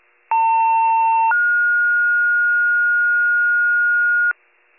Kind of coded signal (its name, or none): two-tone page